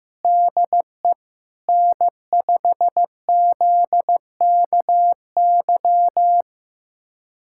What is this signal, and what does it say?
Morse code, heard as a keyed tone, about 15 wpm, 700 Hz. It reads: DE N5ZKY
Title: USA Callsign DE Callsign 15wpm